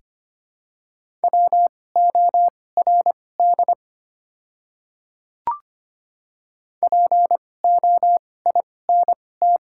Morse code, heard as a keyed tone, about 25 wpm, spaced as for 18 wpm, 700 Hz. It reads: WORD POINT
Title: 2x Character Spacing - Top 200 Words - 25wpm